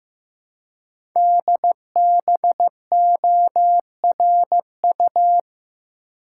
Morse code, heard as a keyed tone, about 15 wpm, 700 Hz. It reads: DBORU
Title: Code Groups - 5 Letters - 15wpm